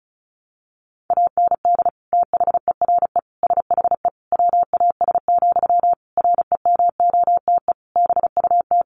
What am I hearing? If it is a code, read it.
Morse code, heard as a keyed tone, about 34 wpm, 700 Hz. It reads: AND THERE SHE WAS, REMOTE BUT
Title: Sentences from Top 1300 Words 35wpm